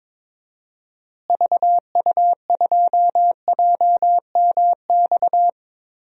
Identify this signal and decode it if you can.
Morse code, heard as a keyed tone, about 21 wpm, 700 Hz. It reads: VU2JMX
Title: DXCC-100 Call Signs - 6 Characters - Repeated 3x - 22wpm